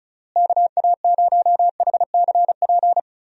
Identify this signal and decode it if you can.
Morse code, heard as a keyed tone, about 34 wpm, 700 Hz. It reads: KA0HCP
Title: USA Call Signs - 6 Characters - Repeated 3x - 35wpm